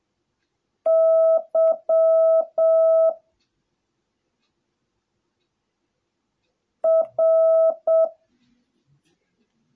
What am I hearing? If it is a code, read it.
Morse code, heard as a keyed tone, about 7 wpm, 650 Hz. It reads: Y R